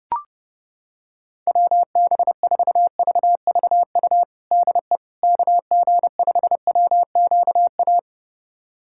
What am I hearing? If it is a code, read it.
Morse code, heard as a keyed tone, about 32 wpm, 700 Hz. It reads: WB4VVU DE KG5WQA
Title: USA Callsign DE Callsign - 6 Characters - 30wpm